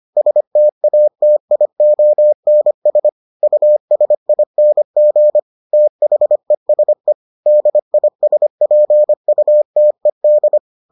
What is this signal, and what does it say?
Morse code, heard as a keyed tone, about 25 wpm, 600 Hz. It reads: STATIONS USING THESE DISPUTED